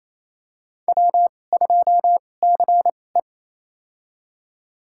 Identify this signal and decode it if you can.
Morse code, heard as a keyed tone, about 28 wpm, spaced as for 20 wpm, 700 Hz. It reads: W2CE
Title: USA Call Signs - 2x Character Spacing - 28wpm